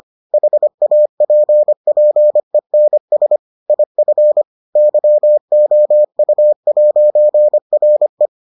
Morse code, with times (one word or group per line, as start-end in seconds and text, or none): 0.3-3.4 HAPPENS
3.7-4.4 IF
4.8-8.3 YOU'RE